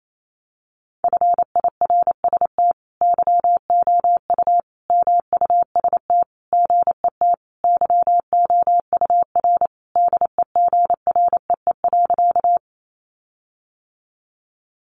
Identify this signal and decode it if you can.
Morse code, heard as a keyed tone, about 28 wpm, 700 Hz. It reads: FIRST YOU MUST GET YOUR DEGREE.
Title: Sentences from Top 1500 Words 28wpm